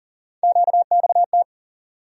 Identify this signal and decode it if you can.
Morse code, heard as a keyed tone, about 40 wpm, 700 Hz. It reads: QXT
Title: USA Call Sign Postfixes - 3 Letters 40wpm